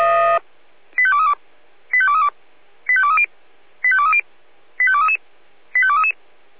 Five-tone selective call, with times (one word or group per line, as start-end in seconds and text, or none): 1.0-1.3 85232
1.9-2.3 85232
2.9-3.2 85230
3.8-4.2 85230
4.8-5.2 85233
5.7-6.1 85233